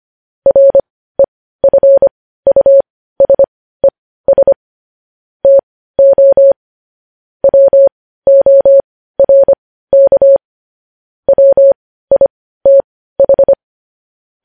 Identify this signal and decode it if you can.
Morse code, heard as a keyed tone, about 25 wpm, spaced as for 15 wpm, 550 Hz. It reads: REFUSES TO WORK WITH